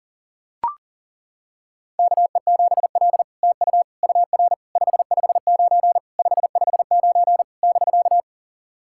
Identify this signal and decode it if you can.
Morse code, heard as a keyed tone, about 40 wpm, 690 Hz. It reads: KE7L TU UR 559 559 <BK>
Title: SOTA QSO Practice 40wpm